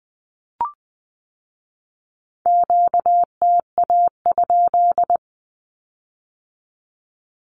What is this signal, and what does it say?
Morse code, heard as a keyed tone, about 20 wpm, 700 Hz. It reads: QTA?
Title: Q-Codes with Spoken Meaning - 20wpm